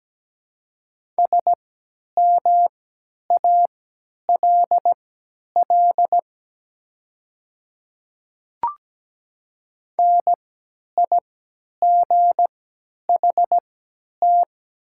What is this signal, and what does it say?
Morse code, heard as a keyed tone, about 17 wpm, spaced as for 10 wpm, 690 Hz. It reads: SMALL NIGHT
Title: 3x Character Spacing - Top 300 Words - 17wpm